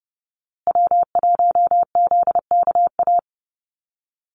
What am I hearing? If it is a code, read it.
Morse code, heard as a keyed tone, about 30 wpm, 700 Hz. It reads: W1ZKA
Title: USA Call Signs Repeated 3x 30wpm